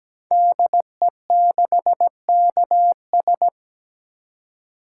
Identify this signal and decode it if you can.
Morse code, heard as a keyed tone, about 17 wpm, 700 Hz. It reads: DE6KS
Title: Code Groups - 5 Letter:Numbers - 17wpm